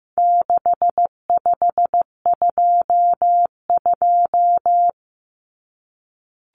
Morse code, heard as a keyed tone, about 15 wpm, 700 Hz. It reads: 6522
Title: Common Cut Numbers - 4-Digit 15wpm